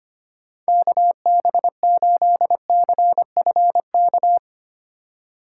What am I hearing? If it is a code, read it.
Morse code, heard as a keyed tone, about 32 wpm, 700 Hz. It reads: KB8CFK